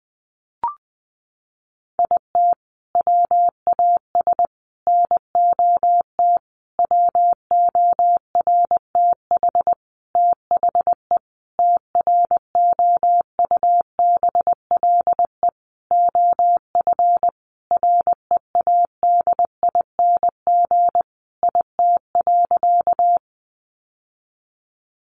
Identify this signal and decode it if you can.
Morse code, heard as a keyed tone, about 20 wpm, 700 Hz. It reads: IT WAS NOT WORTH THE TROUBLE OF READING IT.
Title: Sentences from Top 1200 Words 20wpm